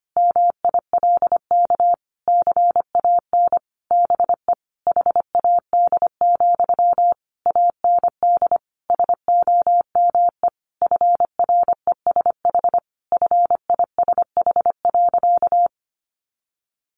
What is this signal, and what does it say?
Morse code, heard as a keyed tone, about 25 wpm, 700 Hz. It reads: MILK CAN BE HAD, AND SOME FRESH FISH.
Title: Sentences from Top 1500 Words 25wpm